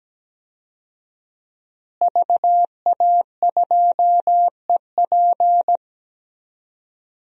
Morse code, heard as a keyed tone, about 17 wpm, 700 Hz. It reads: VA2EP
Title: DXCC-100 Call Signs - 5 Characters - Repeated 2x - 17wpm